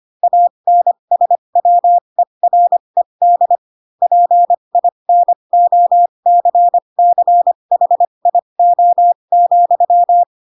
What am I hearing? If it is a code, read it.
Morse code, heard as a keyed tone, about 25 wpm, spaced as for 22 wpm, 700 Hz. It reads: ANSWERED PINOCCHIO,